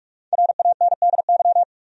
Morse code, heard as a keyed tone, about 43 wpm, 700 Hz. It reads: RANDY
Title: USA Common Names - Max 5 Letters - 45wpm